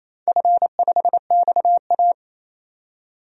Morse code, heard as a keyed tone, about 27 wpm, 700 Hz. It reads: F5XA
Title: DXCC-100 Call Signs - 4 Characters - 28wpm